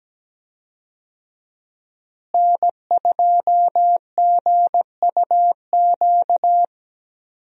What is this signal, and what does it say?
Morse code, heard as a keyed tone, about 17 wpm, 700 Hz. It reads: N2GUQ
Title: USA Call Signs - 5 Characters - Repeated 2x - 17wpm